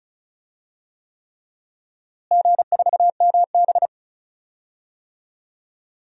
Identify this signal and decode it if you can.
Morse code, heard as a keyed tone, about 36 wpm, 690 Hz. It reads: G4MB
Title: DXCC-100 Call Signs - 4 Characters - Repeated 3x - 35wpm